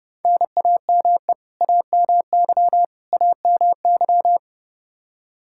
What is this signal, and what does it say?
Morse code, heard as a keyed tone, about 30 wpm, 700 Hz. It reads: NAME AMY AMY
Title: Practice QSO - Exchange 1 - Beginning - 30wpm